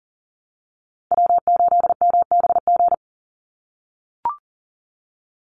Morse code, heard as a keyed tone, about 38 wpm, 700 Hz. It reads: W8MBG
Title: USA Call Signs 40wpm